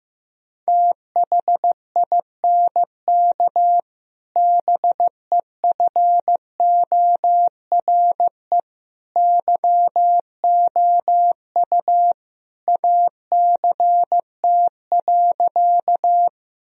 Morse code, heard as a keyed tone, about 15 wpm, 700 Hz. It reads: THINK BEFORE YOU ACT.
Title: Sentences from Top 1000 Words - Review Part 1 - 15wpm